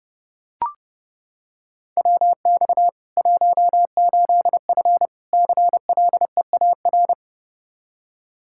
Morse code, heard as a keyed tone, about 30 wpm, 700 Hz. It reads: WX 18F CLEAR
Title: Practice QSO - Exchange 2 - Info - 30wpm